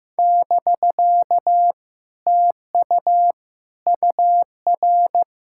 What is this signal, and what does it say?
Morse code, heard as a keyed tone, about 15 wpm, 700 Hz. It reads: <BK> TU UR
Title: Speed-Racing Common QSO Phrases Encore - 15wpm